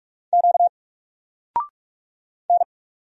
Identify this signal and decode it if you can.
Morse code, heard as a keyed tone, about 45 wpm, 700 Hz. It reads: Q N
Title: Single Letter-Number 45wpm